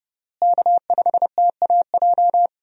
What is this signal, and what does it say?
Morse code, heard as a keyed tone, about 32 wpm, 700 Hz. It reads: K5TAJ
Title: USA Call Signs - 5 Characters - Repeated 3x - 30wpm